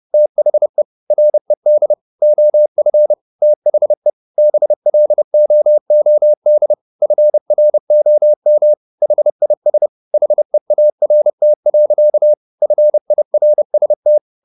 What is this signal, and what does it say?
Morse code, heard as a keyed tone, about 30 wpm, 600 Hz. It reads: THE RED OF THE BLOOD FROM HIS HEART. FIRST